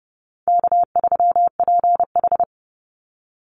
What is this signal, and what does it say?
Morse code, heard as a keyed tone, about 29 wpm, 700 Hz. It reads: K3PH
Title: Call Signs Repeated 2x - 30wpm